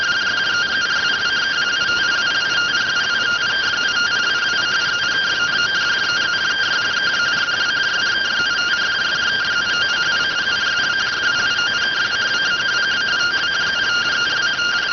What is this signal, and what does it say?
Two-tone teleprinter signal, about 170 Hz shift.